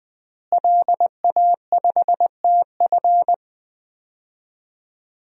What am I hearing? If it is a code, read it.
Morse code, heard as a keyed tone, about 20 wpm, 700 Hz. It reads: LA5TF
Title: DXCC-100 Call Signs - 5 Characters - Repeated 2x - 20wpm